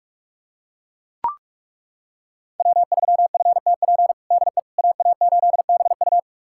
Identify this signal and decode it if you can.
Morse code, heard as a keyed tone, about 41 wpm, 700 Hz. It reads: W3FTP DE AA8BU
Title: USA Callsign DE Callsign - 5 Characters - 45wpm